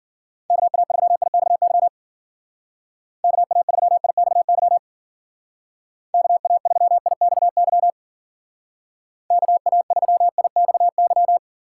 Morse code, heard as a keyed tone, about 51 wpm, 700 Hz. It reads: KA3IXY KA3IXY KA3IXY KA3IXY